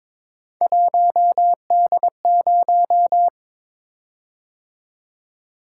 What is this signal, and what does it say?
Morse code, heard as a keyed tone, about 22 wpm, 700 Hz. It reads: 1D0
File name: practice / 22wpm / Three Letter-Number Combinations - 22wpm